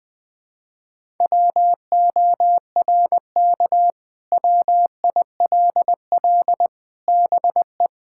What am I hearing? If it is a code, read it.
Morse code, heard as a keyed tone, about 21 wpm, 700 Hz. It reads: WORK WILL BE
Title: Sentences from Top 1000 Words - Review Part 6 - 20wpm